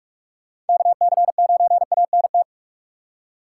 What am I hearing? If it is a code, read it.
Morse code, heard as a keyed tone, about 46 wpm, 700 Hz. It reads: KC9ANT
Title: USA Call Signs - 4x Character Spacing - 45wpm